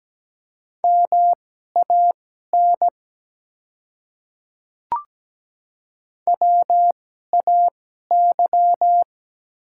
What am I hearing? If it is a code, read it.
Morse code, heard as a keyed tone, about 17 wpm, spaced as for 12 wpm, 690 Hz. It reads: MAN WAY
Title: 2x Character Spacing - Top 300 Words - 17wpm